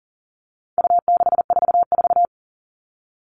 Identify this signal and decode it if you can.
Morse code, heard as a keyed tone, about 40 wpm, 700 Hz. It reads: U644
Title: All Cut Numbers - 4-Digit 40wpm